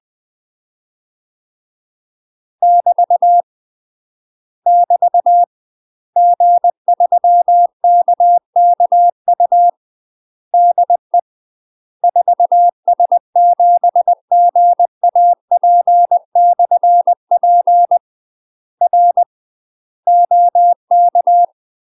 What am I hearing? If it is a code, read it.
Morse code, heard as a keyed tone, about 20 wpm, 700 Hz. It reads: <BT> <BT> G3KKU DE 4S7GAP/P R OK
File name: UK MASTER QSOs TRAINING / 20WPM / UK MASTER QSOs 0050